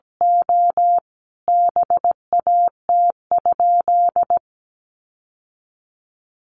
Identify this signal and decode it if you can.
Morse code, heard as a keyed tone, about 17 wpm, 700 Hz. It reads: O BAT?
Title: Sentences from Top 1500 Words 17wpm